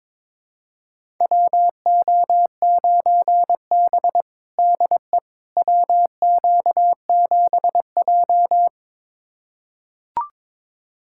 Morse code, heard as a keyed tone, about 23 wpm, 700 Hz. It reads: WO9B DE WQ7J
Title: USA Callsign DE Callsign - 4 Characters - 22wpm